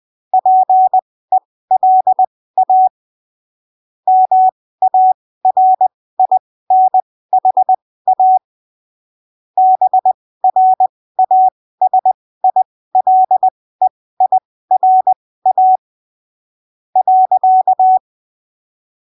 Morse code, heard as a keyed tone, about 20 wpm, spaced as for 14 wpm, 750 Hz. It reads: PELA MARINHA BRASILEIRA.
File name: arquivos / texto / texto 14 20ppm